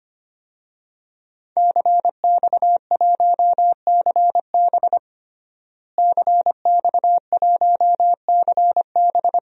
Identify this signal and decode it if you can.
Morse code, heard as a keyed tone, about 25 wpm, 700 Hz. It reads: CX1CB CX1CB
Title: DXCC-100 Call Signs - 5 Characters - Repeated 3x - 25wpm